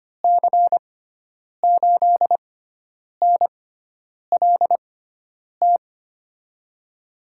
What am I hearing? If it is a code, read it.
Morse code, heard as a keyed tone, about 25 wpm, spaced as for 9 wpm, 700 Hz. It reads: C8NLT
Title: USA Call Signs - 6x Character Spacing - 25wpm